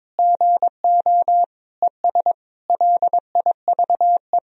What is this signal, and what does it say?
Morse code, heard as a keyed tone, about 22 wpm, 700 Hz. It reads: GO ES LIVE
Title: Sentences from Top 500 Words and Abbreviations 22wpm